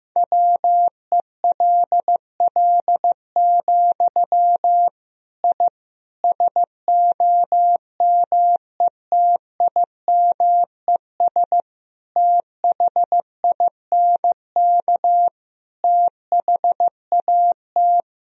Morse code, heard as a keyed tone, about 15 wpm, 700 Hz. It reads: WELL, I SOMETIMES THINK THAT
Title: Sentences from Top 1000 Words - Review Part 3 - 15wpm